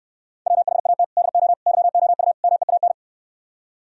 Morse code, heard as a keyed tone, about 68 wpm, 700 Hz. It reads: WHAT DO YOU MEAN